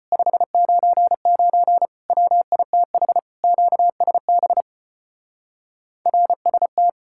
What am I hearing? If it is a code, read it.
Morse code, heard as a keyed tone, about 33 wpm, 700 Hz. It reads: 599 WITH QSB RST